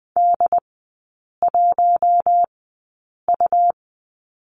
Morse code, heard as a keyed tone, about 20 wpm, spaced as for 8 wpm, 700 Hz. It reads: D1U
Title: 100 Lesson 17e - Letter u - Three Characters 20wpm